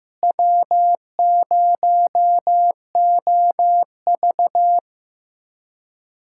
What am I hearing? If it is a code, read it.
Morse code, heard as a keyed tone, about 15 wpm, 700 Hz. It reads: W0OV
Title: USA Callsign DE Callsign - 4 Characters - 15wpm